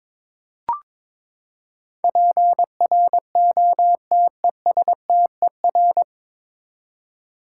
Morse code, heard as a keyed tone, about 22 wpm, 700 Hz. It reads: PROTESTER
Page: Morse Code Ninja practice 22wpm